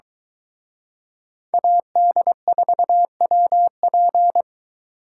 Morse code, heard as a keyed tone, about 22 wpm, 700 Hz. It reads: AD4WP